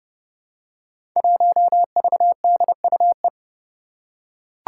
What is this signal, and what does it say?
Morse code, heard as a keyed tone, about 30 wpm, 700 Hz. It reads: 1VDUE
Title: Five Challenging Letter-Number Combinations 30wpm